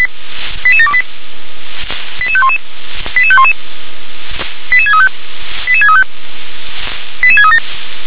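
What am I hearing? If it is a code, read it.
Five-tone selective call (ZVEI), heard as quick runs of five tones, tones of about 70 ms. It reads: 88728 88511 88511 88635 88635 88637
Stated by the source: ZVEI five-tone selective call 88730 received by SDR.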